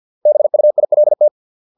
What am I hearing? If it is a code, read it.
Morse code, heard as a keyed tone, about 50 wpm, 600 Hz. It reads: BUILT